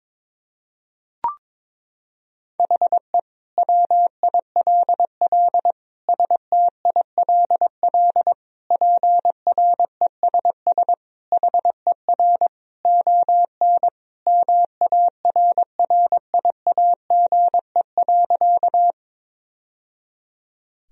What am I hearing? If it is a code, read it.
Morse code, heard as a keyed tone, about 22 wpm, 700 Hz. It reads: HE WILL STILL PRESS HER ON MARRIAGE.
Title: Sentences from Top 1000 Words - Review Part 1 - 22wpm